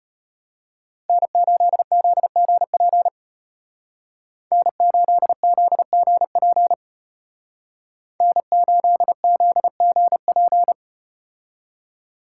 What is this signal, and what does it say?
Morse code, heard as a keyed tone, about 34 wpm, 700 Hz. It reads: N8ZGP N8ZGP N8ZGP